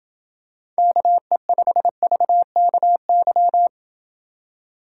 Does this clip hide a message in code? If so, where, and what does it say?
Yes: Morse code from 0.8 to 3.7 s, heard as a keyed tone, about 28 wpm, 700 Hz. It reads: KE5VKY